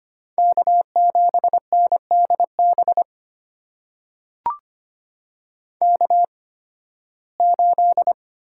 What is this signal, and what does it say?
Morse code, heard as a keyed tone, about 25 wpm, spaced as for 16 wpm, 700 Hz. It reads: K7NDB K8
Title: USA Call Signs - 8x Character Spacing - 25wpm